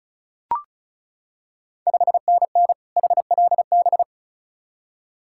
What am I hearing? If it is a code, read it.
Morse code, heard as a keyed tone, about 36 wpm, 690 Hz. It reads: 5NN HLB
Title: Ontario QSO Party Exchange 35wpm